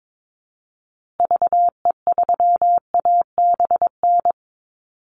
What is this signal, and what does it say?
Morse code, heard as a keyed tone, about 22 wpm, 700 Hz. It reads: VE3ABN
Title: Call Signs Repeated 2x - 22wpm